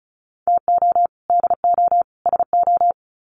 Morse code, heard as a keyed tone, about 34 wpm, 700 Hz. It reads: TO DO SO
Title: Sets of 3 Words 35wpm